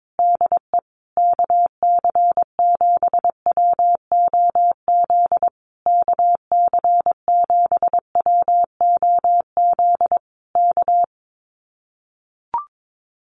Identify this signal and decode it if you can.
Morse code, heard as a keyed tone, about 22 wpm, 700 Hz. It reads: DE KC7WOZ KC7WOZ K
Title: Common QSO Phrases Encore 22wpm